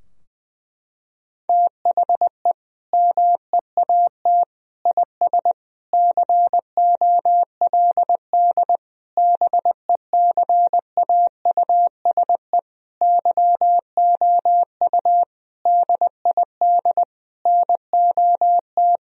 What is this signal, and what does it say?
Morse code, heard as a keyed tone, about 20 wpm, 700 Hz. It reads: THE MEAT IS COLD BECAUSE YOU DID NOT
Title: Sentences from Top 1000 Words - Review Part 3 - 20wpm